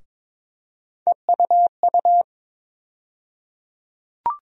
Morse code, heard as a keyed tone, about 22 wpm, 700 Hz. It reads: EUU
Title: USA Call Sign Postfixes - 3 Letters 22wpm